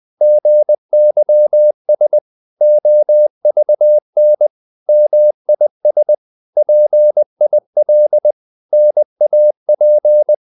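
Morse code, heard as a keyed tone, about 20 wpm, 600 Hz. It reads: GYS OVN MIS PIL NAP
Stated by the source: Morse code practice of short Danish words at 20 wpm, from CW med Gnister.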